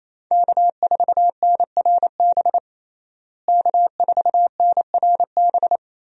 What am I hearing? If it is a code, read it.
Morse code, heard as a keyed tone, about 28 wpm, 700 Hz. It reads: K4NRB K4NRB
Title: USA Call Signs - 5 Characters - Repeated 3x - 28wpm